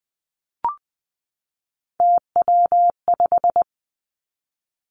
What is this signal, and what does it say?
Morse code, heard as a keyed tone, about 20 wpm, 700 Hz. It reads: TW5
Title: Three Challenging Letter-Number Combinations 20wpm